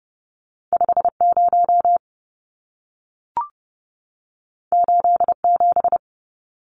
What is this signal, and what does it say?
Morse code, heard as a keyed tone, about 30 wpm, 700 Hz. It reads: 50 87
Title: Numbers - 2-Digit 30wpm